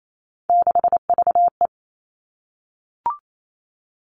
Morse code, heard as a keyed tone, about 28 wpm, 700 Hz. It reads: 6VE
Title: Three Challenging Letter-Number Combinations 28wpm